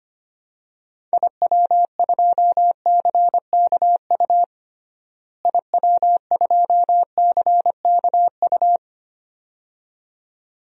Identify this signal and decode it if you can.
Morse code, heard as a keyed tone, about 25 wpm, 700 Hz. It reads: IW2CKU IW2CKU
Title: DXCC-100 Call Signs - 6 Characters - Repeated 2x - 25wpm